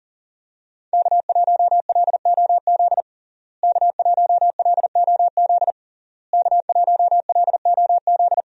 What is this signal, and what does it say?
Morse code, heard as a keyed tone, about 40 wpm, 700 Hz. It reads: K1LOZ K1LOZ K1LOZ
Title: USA Call Signs - 5 Characters - Repeated 3x - 40wpm